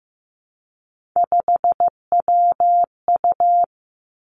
Morse code, heard as a keyed tone, about 15 wpm, 700 Hz. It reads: 5WU
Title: Three Challenging Letter-Number Combinations 15wpm